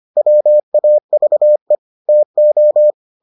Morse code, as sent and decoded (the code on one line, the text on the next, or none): .-- .- ...- . / - ---
WAVE TO